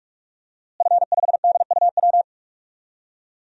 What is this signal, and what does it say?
Morse code, heard as a keyed tone, about 46 wpm, 700 Hz. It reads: F5DUW
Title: DXCC-100 Call Signs - 5 Characters - Repeated 2x - 45wpm